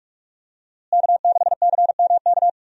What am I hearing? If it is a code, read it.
Morse code, heard as a keyed tone, about 46 wpm, 700 Hz. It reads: K6CMK
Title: USA Call Signs - 5 Characters - Repeated 3x - 45wpm